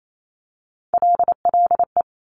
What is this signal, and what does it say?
Morse code, heard as a keyed tone, about 28 wpm, 700 Hz. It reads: LLE
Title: Top Three Letter Combinations in Words 28wpm